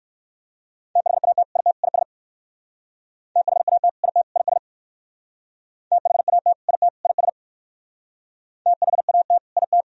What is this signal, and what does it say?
Morse code, heard as a keyed tone, about 58 wpm, 700 Hz. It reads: THAT IT IS THAT IT IS THAT IT IS THAT IT